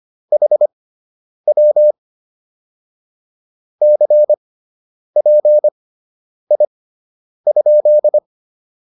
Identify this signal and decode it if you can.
Morse code, heard as a keyed tone, about 25 wpm, spaced as for 9 wpm, 600 Hz. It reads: HW CPI?